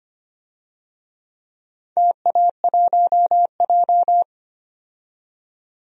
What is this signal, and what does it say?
Morse code, heard as a keyed tone, about 25 wpm, 690 Hz. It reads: TA1J
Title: DXCC-100 Call Signs - 4 Characters - 25wpm